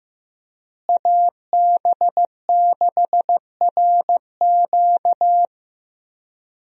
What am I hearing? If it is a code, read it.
Morse code, heard as a keyed tone, about 15 wpm, 700 Hz. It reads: AB6RQ
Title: USA Call Signs - 5 Characters - Repeated 3x - 15wpm